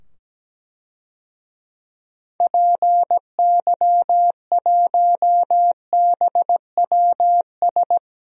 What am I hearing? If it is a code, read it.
Morse code, heard as a keyed tone, about 17 wpm, 700 Hz. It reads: PY1BWS
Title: DXCC-100 Call Signs - 6 Characters - 17wpm